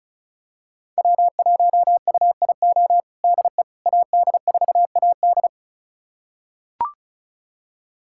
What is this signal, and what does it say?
Morse code, heard as a keyed tone, about 34 wpm, 700 Hz. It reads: W1UIO DE AD4AD
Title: USA Callsign DE Callsign - 5 Characters - 35wpm